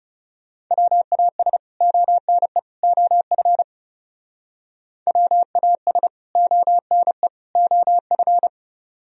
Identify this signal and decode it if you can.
Morse code, heard as a keyed tone, about 37 wpm, 700 Hz. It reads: WAS ONE OF WAS ONE OF
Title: Speed-Racing - Sets of 3 Words 30wpm